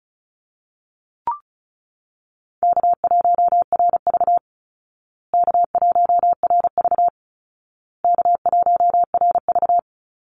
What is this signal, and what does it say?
Morse code, heard as a keyed tone, about 34 wpm, 700 Hz. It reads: K1RV K1RV K1RV
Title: Easy Call Signs Repeated 3x - 35wpm